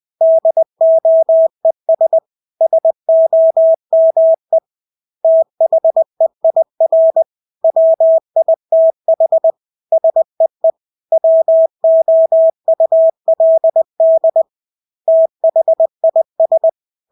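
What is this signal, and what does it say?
Morse code, heard as a keyed tone, about 20 wpm, 650 Hz. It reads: DOES SOME THEIR WITH SEE WOULD THIS
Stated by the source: Morse practice of English words at 20 wpm.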